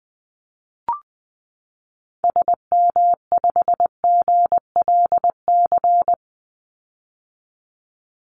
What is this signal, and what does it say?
Morse code, heard as a keyed tone, about 20 wpm, 700 Hz. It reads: SM5GLC
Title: Call Signs - 20wpm